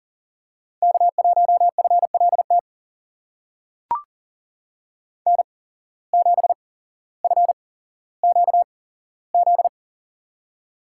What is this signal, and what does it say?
Morse code, heard as a keyed tone, about 39 wpm, spaced as for 17 wpm, 700 Hz. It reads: K1FLT N7FQZ
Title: USA Call Signs - 5 Characters - 8x Character Spacing - 40wpm